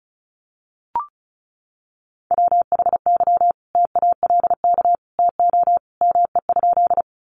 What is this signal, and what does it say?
Morse code, heard as a keyed tone, about 34 wpm, 700 Hz. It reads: WHY TALK TO ME?
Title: Sentences from Top 400 Words 35wpm